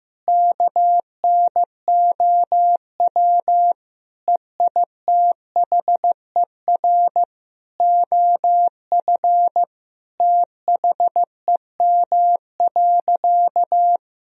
Morse code, heard as a keyed tone, about 15 wpm, 700 Hz. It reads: KNOW EITHER OF THEM.
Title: Sentences from Top 1000 Words - Review Part 4 - 15wpm